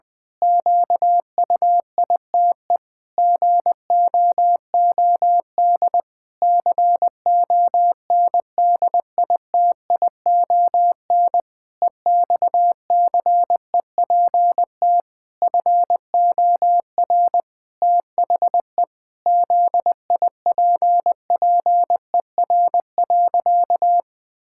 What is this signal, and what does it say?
Morse code, heard as a keyed tone, about 21 wpm, 700 Hz. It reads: QUITE GOOD CONDITION EXCEPT FOR THE ZIPPER.
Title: Pangram Challenge v2 - 20wpm